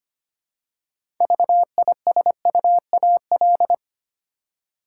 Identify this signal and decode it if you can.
Morse code, heard as a keyed tone, about 25 wpm, 700 Hz. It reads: VISUAL
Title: Six Letter Words - 25wpm